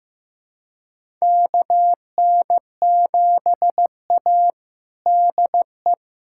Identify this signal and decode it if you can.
Morse code, heard as a keyed tone, about 15 wpm, 700 Hz. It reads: KN7A DE
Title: USA Callsign DE Callsign - 4 Characters - 15wpm